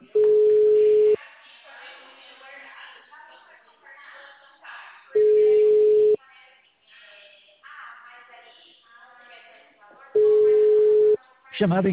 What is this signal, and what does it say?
Ringback tone (Europe, 425 Hz), with speech.